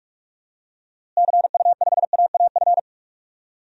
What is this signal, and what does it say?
Morse code, heard as a keyed tone, about 46 wpm, 700 Hz. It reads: CU5AAF
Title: DXCC-100 Call Signs - 6 Characters - Repeated 3x - 45wpm